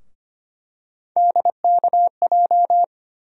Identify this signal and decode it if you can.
Morse code, heard as a keyed tone, about 25 wpm, 700 Hz. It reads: DKJ
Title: USA Call Sign Postfixes - 3 Letters 25wpm